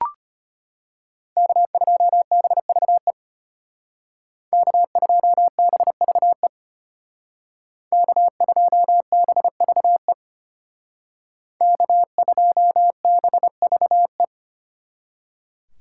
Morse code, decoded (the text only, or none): K2BVE K2BVE K2BVE K2BVE